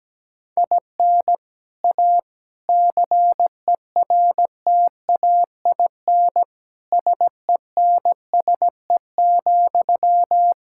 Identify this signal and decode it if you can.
Morse code, heard as a keyed tone, about 17 wpm, 700 Hz. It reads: IN A CERTAIN SENSE,
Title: Sentences from Top 1000 Words - Review Part 6 - 17wpm